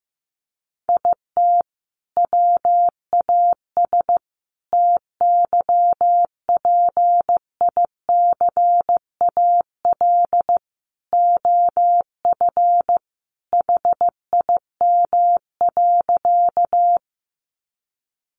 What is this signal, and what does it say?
Morse code, heard as a keyed tone, about 15 wpm, 700 Hz. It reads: IT WAS TYPICAL OF HIM.
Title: Sentences from Top 1100 Words 15wpm